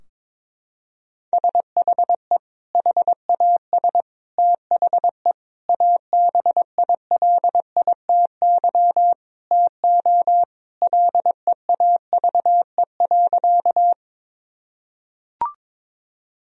Morse code, heard as a keyed tone, about 22 wpm, 700 Hz. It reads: SHE HAS THE ABILITY TO LEAVE.
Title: Sentences from Top 1000 Words - Review Part 3 - 22wpm